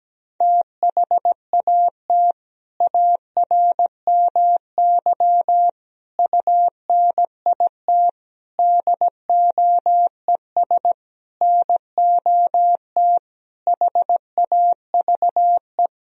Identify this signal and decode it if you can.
Morse code, heard as a keyed tone, about 17 wpm, 700 Hz. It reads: THAT ARMY UNIT DOES NOT HAVE
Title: Sentences from Top 1000 Words - Review Part 2 - 17wpm